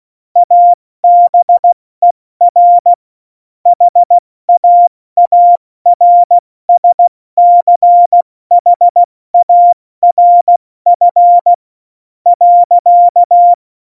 Morse code, heard as a keyed tone, about 16 wpm, 700 Hz. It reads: ABER HAARSCHARF .
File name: Morsekurs / Witze / Witz 29. September 2020, 70 F80